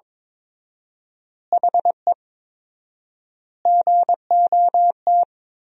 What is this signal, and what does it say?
Morse code, heard as a keyed tone, about 22 wpm, 700 Hz. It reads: HE GOT
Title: 4x Word-Spacing - Sentences from Top 300 Words - 22wpm